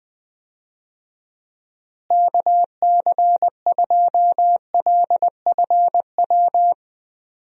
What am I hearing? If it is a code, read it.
Morse code, heard as a keyed tone, about 20 wpm, 700 Hz. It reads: KC2LFW
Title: USA Call Signs - 6 Characters - Repeated 2x - 20wpm